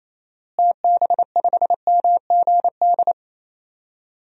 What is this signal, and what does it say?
Morse code, heard as a keyed tone, about 28 wpm, 700 Hz. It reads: TB5MGD
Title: DXCC-100 Call Signs - 6 Characters - Repeated 2x - 28wpm